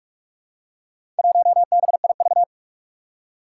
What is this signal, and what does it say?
Morse code, heard as a keyed tone, about 42 wpm, 700 Hz. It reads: 1BIV
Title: Four Challenging Letter-Number Combinations 45wpm